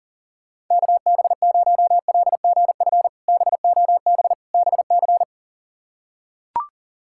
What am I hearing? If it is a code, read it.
Morse code, heard as a keyed tone, about 40 wpm, 700 Hz. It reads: KB0LGF BOB BC
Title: K1USN SST QSO Practice - 40wpm